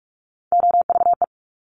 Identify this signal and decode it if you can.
Morse code, heard as a keyed tone, about 46 wpm, 700 Hz. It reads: OVE
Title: Top Three Letter Combinations in Words 45wpm